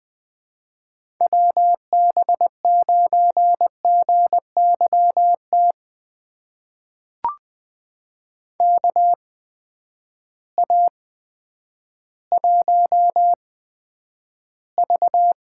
Morse code, heard as a keyed tone, about 20 wpm, spaced as for 10 wpm, 700 Hz. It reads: WB9GYT KA1V
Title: USA Call Signs - 6 Characters - 8x Character Spacing - 20wpm